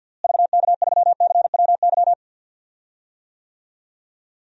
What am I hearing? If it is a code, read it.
Morse code, heard as a keyed tone, about 49 wpm, 700 Hz. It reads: VK3CWY
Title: DXCC-100 Call Signs - 6 Characters - Repeated 2x - 50wpm